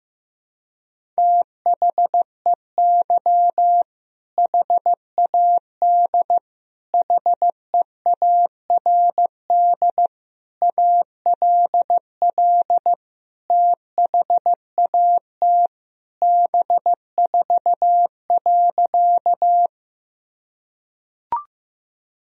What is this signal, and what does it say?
Morse code, heard as a keyed tone, about 15 wpm, 700 Hz. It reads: THEY HAD HEARD ALL THAT B4.
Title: Sentences from Top 400 Words and Abbreviations 15wpm